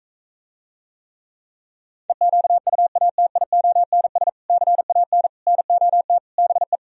Morse code, heard as a keyed tone, about 46 wpm, 700 Hz. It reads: EQUATIONS CAN NOT BE